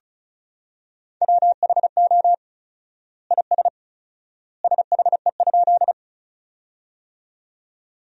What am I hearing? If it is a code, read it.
Morse code, heard as a keyed tone, about 34 wpm, 700 Hz. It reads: WHO IS SHE?
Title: 4x Word-Spacing - Sentences from Top 100 Words - 35wpm